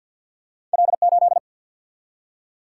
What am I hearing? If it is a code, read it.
Morse code, heard as a keyed tone, about 50 wpm, 700 Hz. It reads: L8
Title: Single-Letter Single-Number 50wpm